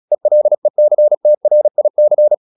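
Morse code, heard as a keyed tone, about 36 wpm, 600 Hz. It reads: ELECTRIC
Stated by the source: Morse practice at 36 wpm, text of Ken Follett's novel Never, from CW med Gnister.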